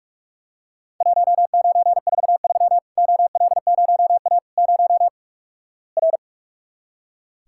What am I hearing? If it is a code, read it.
Morse code, heard as a keyed tone, about 45 wpm, 700 Hz. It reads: 1943 OL0A 0 R
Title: Worked All Europe DX Contest - QTC Exchange - 45wpm